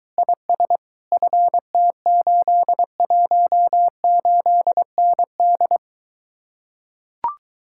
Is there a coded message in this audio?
Yes: Morse code from 0.2 to 5.8 s, heard as a keyed tone, about 23 wpm, 700 Hz. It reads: IS FT818ND